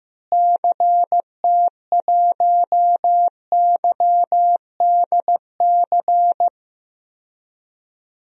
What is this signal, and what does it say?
Morse code, heard as a keyed tone, about 15 wpm, 700 Hz. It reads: CT1YDC
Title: DXCC-100 Call Signs - 6 Characters - Repeated 3x - 15wpm